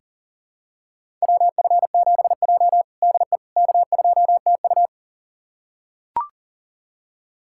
Morse code, heard as a keyed tone, about 37 wpm, 700 Hz. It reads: WF7J DE K2TU
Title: USA Callsign DE Callsign - 4 Characters - 40wpm